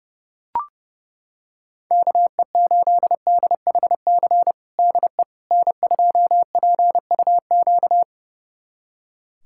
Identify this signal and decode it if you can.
Morse code, heard as a keyed tone, about 32 wpm, 700 Hz. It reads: KE8DHC DE N2PUQ
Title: USA Callsign DE Callsign 30wpm